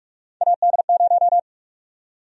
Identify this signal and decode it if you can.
Morse code, heard as a keyed tone, about 46 wpm, 700 Hz. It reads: AD0
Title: USA Call Sign Prefixes - Repeated 2x - 45wpm